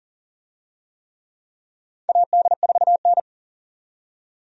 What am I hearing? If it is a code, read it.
Morse code, heard as a keyed tone, about 40 wpm, 700 Hz. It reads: AD4N
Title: USA Call Signs - 4 Characters - 40wpm